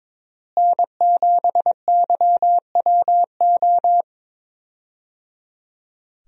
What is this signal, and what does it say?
Morse code, heard as a keyed tone, about 22 wpm, 700 Hz. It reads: N7YWO
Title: USA Call Signs - 5 Characters - Repeated 3x - 22wpm